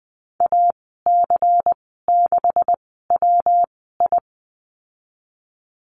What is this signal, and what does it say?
Morse code, heard as a keyed tone, about 20 wpm, spaced as for 15 wpm, 700 Hz. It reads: AC6WI
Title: 179 Lesson 29g - Slash - Call Signs 20wpm